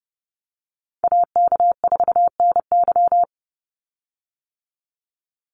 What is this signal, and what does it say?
Morse code, heard as a keyed tone, about 30 wpm, 700 Hz. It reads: AK4NY
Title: Call Signs - 30wpm